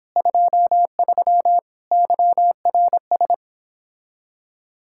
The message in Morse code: 23 YRS

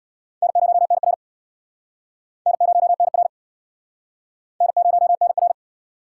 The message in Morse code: N0NR N0NR N0NR

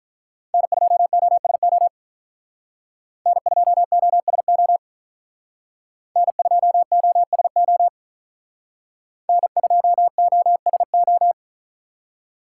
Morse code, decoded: N2OSO N2OSO N2OSO N2OSO

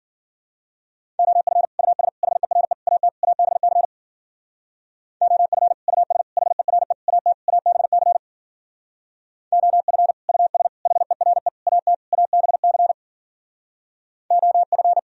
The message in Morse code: OF US HERE AT ABC OF US HERE AT ABC OF US HERE AT ABC OF

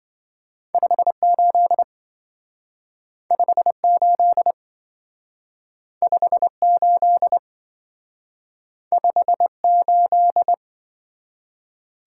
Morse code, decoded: 58 58 58 58

